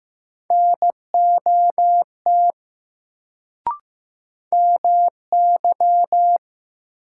Morse code, heard as a keyed tone, about 15 wpm, 700 Hz. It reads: NOT MY
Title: Sets of 2 Words - New Word Drill - 15wpm